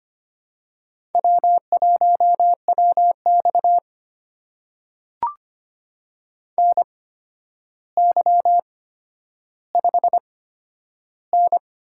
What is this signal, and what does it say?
Morse code, heard as a keyed tone, about 25 wpm, spaced as for 11 wpm, 700 Hz. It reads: W1WX NY5N
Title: USA Call Signs - 4 Characters - 8x Character Spacing - 25wpm